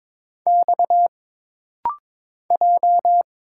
Morse code, heard as a keyed tone, about 22 wpm, 700 Hz. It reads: X J
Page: This rapid-fire timing is geared to push your ICR (Instant Character Recognition) skills.